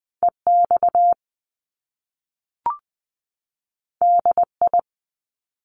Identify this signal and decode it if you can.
Morse code, heard as a keyed tone, about 20 wpm, 700 Hz. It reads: E X D I
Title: Top Two Letter Combinations in Words 20wpm